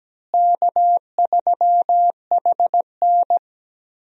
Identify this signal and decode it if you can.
Morse code, heard as a keyed tone, about 17 wpm, 700 Hz. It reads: K3HN
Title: USA Call Signs - 4 Characters - Repeated 3x - 17wpm